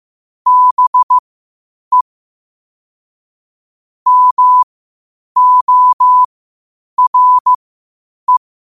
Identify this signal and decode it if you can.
Morse code, heard as a keyed tone, about 15 wpm, spaced as for 8 wpm, 1000 Hz. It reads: BE MORE
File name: qotdfiles / 8wpm